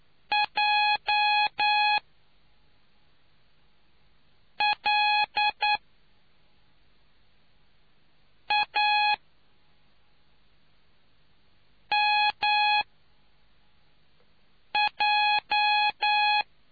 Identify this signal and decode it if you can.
Morse code, heard as a keyed tone, about 10 wpm, spaced as for 4 wpm, 840 Hz. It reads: J L A MJ